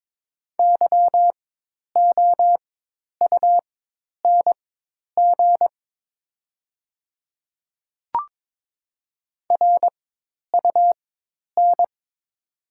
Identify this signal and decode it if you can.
Morse code, heard as a keyed tone, about 22 wpm, spaced as for 10 wpm, 700 Hz. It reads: YOUNG RUN
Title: 4x Character Spacing - Top 500 Words - 22wpm